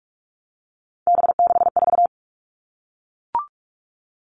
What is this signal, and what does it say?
Morse code, heard as a keyed tone, about 44 wpm, 700 Hz. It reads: B64
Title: All Cut Numbers - 3-Digit 45wpm